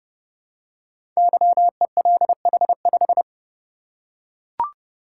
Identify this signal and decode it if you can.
Morse code, heard as a keyed tone, about 30 wpm, 700 Hz. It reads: YELH5
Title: Code Groups - 5 Letter:Numbers - 30wpm